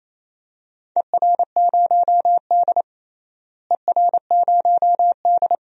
Morse code, heard as a keyed tone, about 28 wpm, 700 Hz. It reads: ER0D ER0D
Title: DXCC-100 Call Signs - 4 Characters - Repeated 3x - 28wpm